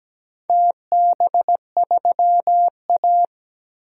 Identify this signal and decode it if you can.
Morse code, heard as a keyed tone, about 17 wpm, 700 Hz. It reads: TB3A
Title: DXCC-100 Call Signs - 4 Characters - Repeated 2x - 17wpm